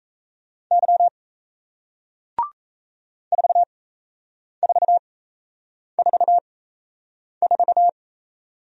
Morse code, heard as a keyed tone, about 38 wpm, spaced as for 8 wpm, 700 Hz. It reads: Y 4444